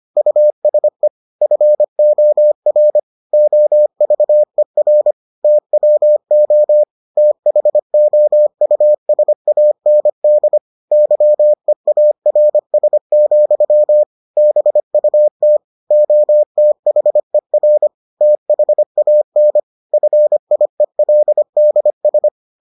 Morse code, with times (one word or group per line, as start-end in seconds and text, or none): 0.2-1.1 USE
1.4-3.0 FOR
3.3-5.1 OVER
5.4-6.8 TWO
7.2-10.6 THOUSAND
10.9-14.0 YEARS,
14.4-15.6 BUT
15.9-17.9 OTHER
18.2-19.6 THAN
19.9-22.3 FIELDS